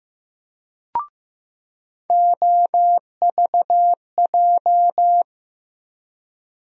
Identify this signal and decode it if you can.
Morse code, heard as a keyed tone, about 15 wpm, 700 Hz. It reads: OVJ
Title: USA Call Sign Postfixes - 3 Letters 15wpm